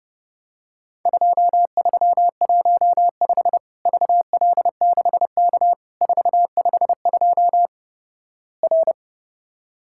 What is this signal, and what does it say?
Morse code, heard as a keyed tone, about 30 wpm, 700 Hz. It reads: 2315 VL6K 452 R
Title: Worked All Europe DX Contest - QTC Exchange - 30wpm